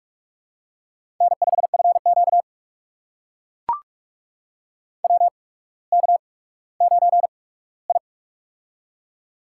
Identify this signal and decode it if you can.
Morse code, heard as a keyed tone, about 44 wpm, spaced as for 19 wpm, 690 Hz. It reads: N5FQ WK9I
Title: USA Call Signs - 4 Characters - 8x Character Spacing - 45wpm